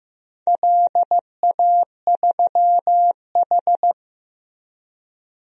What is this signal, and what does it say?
Morse code, heard as a keyed tone, about 15 wpm, 700 Hz. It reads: LA3H
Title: DXCC-100 Call Signs - 4 Characters - Repeated 3x - 15wpm